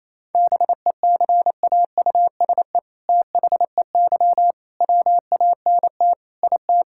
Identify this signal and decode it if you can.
Morse code, heard as a keyed tone, about 28 wpm, 700 Hz. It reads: BECAUSE THEY WANT IT
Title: Sentences from Top 1000 Words - Review Part 5 - 28wpm